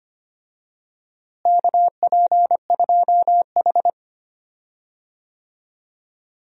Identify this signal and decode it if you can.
Morse code, heard as a keyed tone, about 25 wpm, 700 Hz. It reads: KP2H
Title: DXCC-100 Call Signs - 4 Characters - Repeated 2x - 25wpm